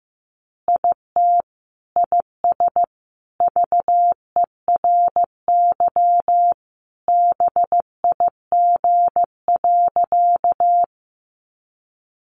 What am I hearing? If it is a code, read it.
Morse code, heard as a keyed tone, about 15 wpm, 700 Hz. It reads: IT IS VERY BIG.
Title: Easy Phrases 15wpm